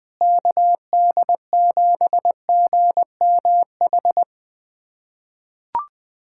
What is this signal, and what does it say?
Morse code, heard as a keyed tone, about 20 wpm, 700 Hz. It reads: KD7GMH
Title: USA Callsign DE Callsign - 6 Characters - 20wpm